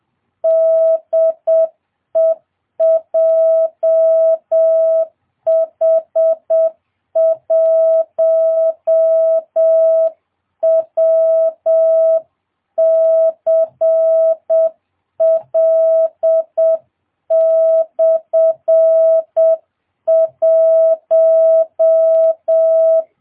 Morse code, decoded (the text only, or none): DE JH1WCL/1